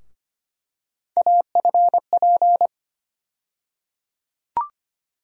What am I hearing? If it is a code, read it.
Morse code, heard as a keyed tone, about 25 wpm, 700 Hz. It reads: AFP
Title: USA Call Sign Postfixes - 3 Letters 25wpm